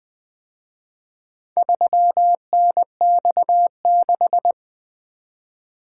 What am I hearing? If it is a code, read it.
Morse code, heard as a keyed tone, about 24 wpm, 700 Hz. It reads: 3NX6
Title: Speed-Racing - Four Letter-Number Combinations - 20wpm